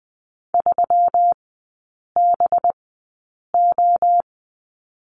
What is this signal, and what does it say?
Morse code, heard as a keyed tone, about 20 wpm, spaced as for 8 wpm, 690 Hz. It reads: 3BO
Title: 212 Lesson 34e - Number 8 - Three Characters 20wpm